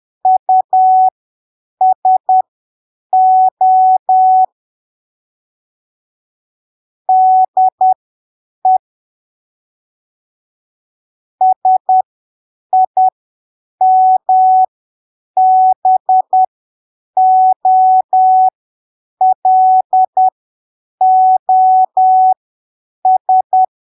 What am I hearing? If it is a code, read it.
Morse code, heard as a keyed tone, about 10 wpm, spaced as for 7 wpm, 750 Hz. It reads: USO DE SIMBOLOS
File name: arquivos / texto / texto 18 10ppm